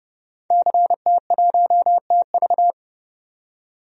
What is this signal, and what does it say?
Morse code, heard as a keyed tone, about 30 wpm, 700 Hz. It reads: CT1TV
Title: DXCC-100 Call Signs - 5 Characters - 30wpm